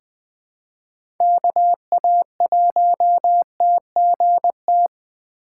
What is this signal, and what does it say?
Morse code, heard as a keyed tone, about 20 wpm, 700 Hz. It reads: KA1TGT
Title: USA Call Signs - 6 Characters - 4x Character Spacing - 20wpm